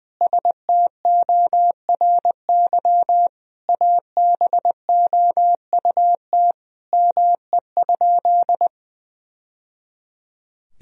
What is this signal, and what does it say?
Morse code, heard as a keyed tone, about 20 wpm, 700 Hz. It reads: STORY ABOUT ME?
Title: Sentences from Top 1000 Words - Review Part 1 - 20wpm